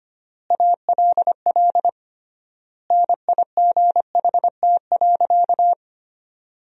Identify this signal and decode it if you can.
Morse code, heard as a keyed tone, about 25 wpm, 700 Hz. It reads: ALL NIGHT.
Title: 3x Word-Spacing - Sentences from Top 300 Words - 25wpm